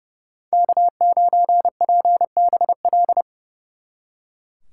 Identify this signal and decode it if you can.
Morse code, heard as a keyed tone, about 38 wpm, 700 Hz. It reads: K9PBL